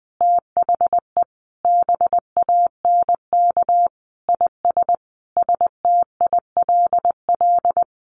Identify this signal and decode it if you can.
Morse code, heard as a keyed tone, about 19 wpm, 700 Hz. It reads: THE BANK IS STILL
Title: Sentences from Top 1500 Words 20wpm